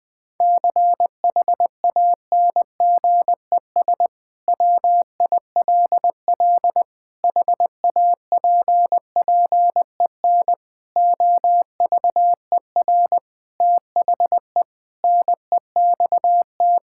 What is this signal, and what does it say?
Morse code, heard as a keyed tone, about 20 wpm, 700 Hz. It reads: CHANGES WILL HAPPEN OVER THE NEXT
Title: Sentences with 5-Word Phrases - 20wpm